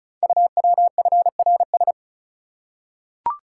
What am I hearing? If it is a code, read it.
Morse code, heard as a keyed tone, about 36 wpm, 700 Hz. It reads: UWFRS
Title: Code Groups - 5 Letters - 35wpm